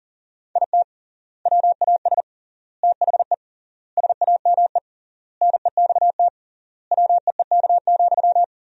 Morse code, heard as a keyed tone, about 40 wpm, 700 Hz. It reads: IT WAS THE SAME NEXT WEEK,
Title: 3x Word-Spacing - Sentences from Top 500 Words - 40wpm